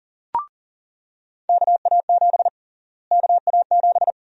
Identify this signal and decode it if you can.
Morse code, heard as a keyed tone, about 40 wpm, 700 Hz. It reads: KA7 KA7
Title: USA Call Sign Prefixes - Repeated 2x - 40wpm